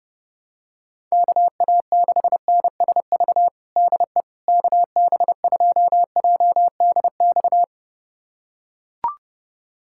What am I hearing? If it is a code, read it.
Morse code, heard as a keyed tone, about 29 wpm, 700 Hz. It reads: KA6NSV DE KB2JDX
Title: USA Callsign DE Callsign 30wpm